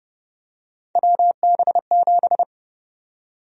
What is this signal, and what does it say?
Morse code, heard as a keyed tone, about 30 wpm, 700 Hz. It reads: WB7
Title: USA Call Sign Prefixes - 30wpm